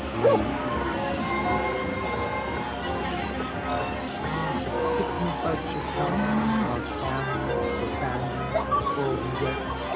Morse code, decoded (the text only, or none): none